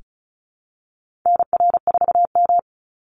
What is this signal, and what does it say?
Morse code, heard as a keyed tone, about 34 wpm, 700 Hz. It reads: NR4M
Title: Call Signs Repeated 2x - 35wpm